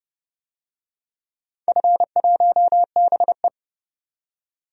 Morse code, heard as a keyed tone, about 30 wpm, 700 Hz. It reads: F1BE
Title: DXCC-100 Call Signs - 4 Characters - 30wpm